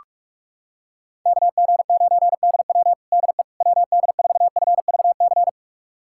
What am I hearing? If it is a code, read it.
Morse code, heard as a keyed tone, about 46 wpm, 690 Hz. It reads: KG9DW DE WD4FVC
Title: USA Callsign DE Callsign 45wpm